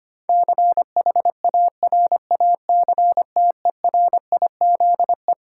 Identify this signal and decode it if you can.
Morse code, heard as a keyed tone, about 25 wpm, 700 Hz. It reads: CHARACTERIZE
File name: Twelve Letter Words - 25wpm